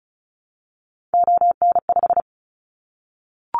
Morse code, heard as a keyed tone, about 34 wpm, 700 Hz. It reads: ON5
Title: Callsign Prefixes for Top 25 DXCC - Repeated 3x - 35wpm